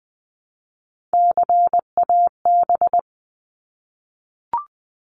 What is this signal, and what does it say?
Morse code, heard as a keyed tone, about 20 wpm, 700 Hz. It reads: CAB
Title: 260 Lesson 42e - Words 20-20wpm